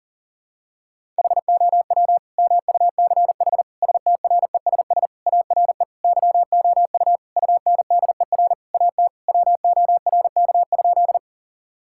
Morse code, heard as a keyed tone, about 37 wpm, 700 Hz. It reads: HOW MUCH STRESS ARE YOU UNDER AT WORK?
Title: Sentences from Top 1000 Words - Review Part 6 - 40wpm